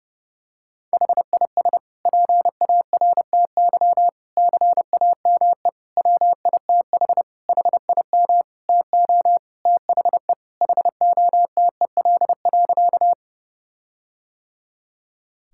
Morse code, heard as a keyed tone, about 30 wpm, 700 Hz. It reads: HIS PARTY CAME WITH HIM TO THE HOTEL.
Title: Sentences from Top 1600 Words 30wpm